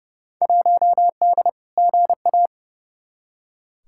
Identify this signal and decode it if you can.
Morse code, heard as a keyed tone, about 31 wpm, 700 Hz. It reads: 1D GA
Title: Field Day Exchanges - 30wpm